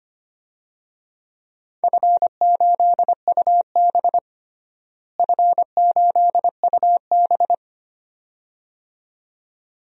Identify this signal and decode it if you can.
Morse code, heard as a keyed tone, about 25 wpm, 700 Hz. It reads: F8UB F8UB
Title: DXCC-100 Call Signs - 4 Characters - Repeated 2x - 25wpm